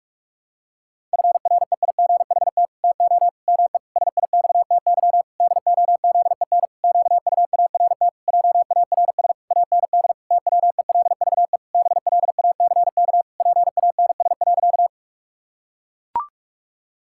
Morse code, heard as a keyed tone, about 44 wpm, 700 Hz. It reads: FREIGHT TO ME SIXTY DOZEN QUART JARS AND TWELVE BLACK PANS.
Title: Pangram Challenge v2 - 45wpm